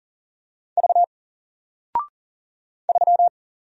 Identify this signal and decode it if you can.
Morse code, heard as a keyed tone, about 40 wpm, 700 Hz. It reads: V 3
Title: Single Character (Letter:Number:Punctuation:Prosign) Alt 40wpm